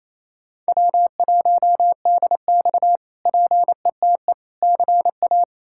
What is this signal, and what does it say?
Morse code, heard as a keyed tone, about 27 wpm, 700 Hz. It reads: W1DX PETE CA
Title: K1USN SST QSO Practice - 28wpm